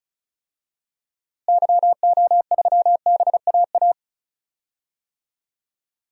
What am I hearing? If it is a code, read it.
Morse code, heard as a keyed tone, about 35 wpm, 700 Hz. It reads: YO3BAA
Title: DXCC-100 Call Signs - 6 Characters - Repeated 3x - 35wpm